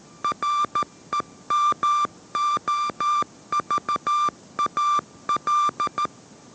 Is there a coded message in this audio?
Yes: Morse code from 0.2 to 6.1 s, heard as a keyed tone, about 13 wpm, 1180 Hz. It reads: REMOVAL